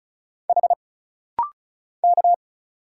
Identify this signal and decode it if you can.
Morse code, heard as a keyed tone, about 36 wpm, 700 Hz. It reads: H K
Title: Single Letter-Number - Rapid-Fire 35wpm